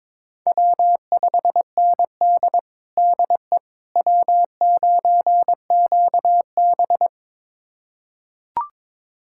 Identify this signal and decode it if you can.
Morse code, heard as a keyed tone, about 22 wpm, 700 Hz. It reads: W5ND DE W9QB
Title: USA Callsign DE Callsign - 4 Characters - 22wpm